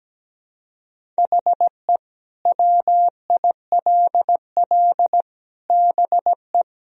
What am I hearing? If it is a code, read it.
Morse code, heard as a keyed tone, about 17 wpm, 700 Hz. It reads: HE WILL BE
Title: Sentences from Top 1000 Words - Review Part 1 - 17wpm